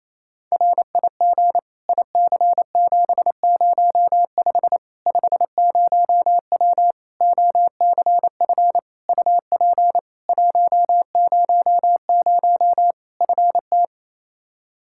Morse code, heard as a keyed tone, about 28 wpm, 700 Hz. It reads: RIG IC705 50W OCF UP 100 FT
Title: Practice QSO - Exchange 2 - Info - 28wpm